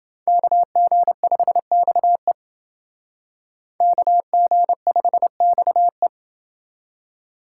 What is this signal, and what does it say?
Morse code, heard as a keyed tone, about 28 wpm, 700 Hz. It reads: KG5XE KG5XE